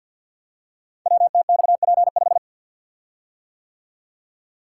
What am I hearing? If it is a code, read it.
Morse code, heard as a keyed tone, about 50 wpm, 700 Hz. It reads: WTXP5
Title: Code Groups - 5 Letter:Numbers - 50wpm